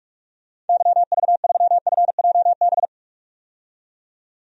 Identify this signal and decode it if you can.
Morse code, heard as a keyed tone, about 46 wpm, 700 Hz. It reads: YV3FJB
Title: DXCC-100 Call Signs - 6 Characters - 45wpm